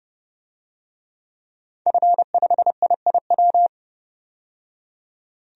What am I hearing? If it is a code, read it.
Morse code, heard as a keyed tone, about 32 wpm, 700 Hz. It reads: F5IIW
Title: DXCC-100 Call Signs - 5 Characters - 30wpm